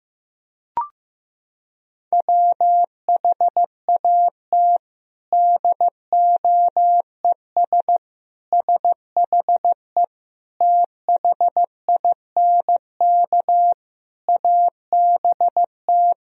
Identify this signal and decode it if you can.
Morse code, heard as a keyed tone, about 15 wpm, 700 Hz. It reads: WHAT DOES SHE THINK ABT
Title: Sentences from Top 200 Words and Abbreviations 15wpm